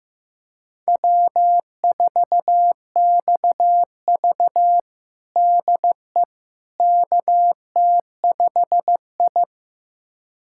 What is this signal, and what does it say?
Morse code, heard as a keyed tone, about 15 wpm, 700 Hz. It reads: W4XV DE KT5I
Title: USA Callsign DE Callsign - 4 Characters - 15wpm